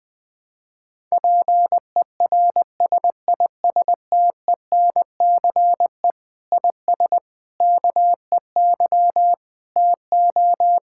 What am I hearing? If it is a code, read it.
Morse code, heard as a keyed tone, about 20 wpm, 700 Hz. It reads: PERSISTENCE IS KEY TO
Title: Difficult Sentences with Hints - 20wpm